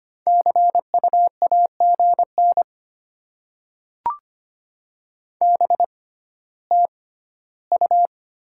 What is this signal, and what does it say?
Morse code, heard as a keyed tone, about 25 wpm, spaced as for 14 wpm, 700 Hz. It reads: CU AGN BTU
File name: Top 20 QSO Elements - 6x Character Spacing - 25wpm